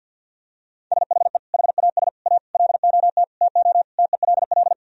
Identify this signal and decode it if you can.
Morse code, heard as a keyed tone, about 51 wpm, 700 Hz. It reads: SHE HAS A LOT TO TELL
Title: Sentences from Top 1000 Words - Review Part 3 - 50wpm